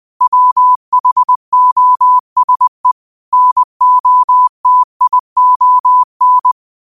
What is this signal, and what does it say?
Morse code, heard as a keyed tone, about 20 wpm, 1000 Hz. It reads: WHOSE NOTION
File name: qotdfiles / 20wpm